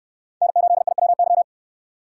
Morse code, heard as a keyed tone, about 69 wpm, 700 Hz. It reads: N8IWY